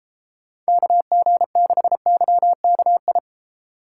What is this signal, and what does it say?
Morse code, heard as a keyed tone, about 33 wpm, 700 Hz. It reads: KG6YKI